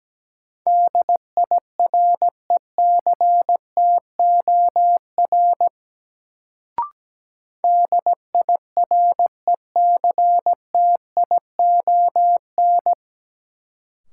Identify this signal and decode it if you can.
Morse code, heard as a keyed tone, about 17 wpm, 700 Hz. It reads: DIRECTOR DIRECTION
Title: Sentences from Top 1300 Words - New Word Drill Alt - 17wpm